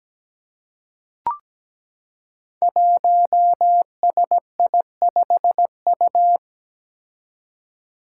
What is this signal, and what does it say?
Morse code, heard as a keyed tone, about 17 wpm, 700 Hz. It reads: 1SI5U
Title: Five Challenging Letter-Number Combinations 17wpm